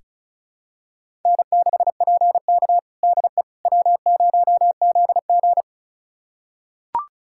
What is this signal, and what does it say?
Morse code, heard as a keyed tone, about 34 wpm, 700 Hz. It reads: N6PK DE W0ZG
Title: USA Callsign DE Callsign - 4 Characters - 35wpm